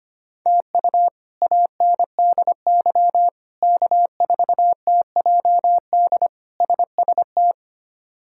Morse code, heard as a keyed tone, about 25 wpm, 700 Hz. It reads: TU ANDY K4TJD SST
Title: K1USN SST QSO Practice - 25wpm